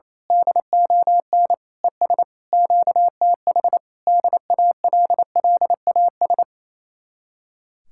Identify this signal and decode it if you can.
Morse code, heard as a keyed tone, about 26 wpm, 700 Hz. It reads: DON ES QTH DALLAS
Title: Speed-Racing Common QSO Phrases Encore - 28wpm